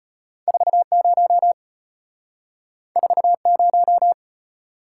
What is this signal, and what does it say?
Morse code, heard as a keyed tone, about 32 wpm, 700 Hz. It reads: 40 40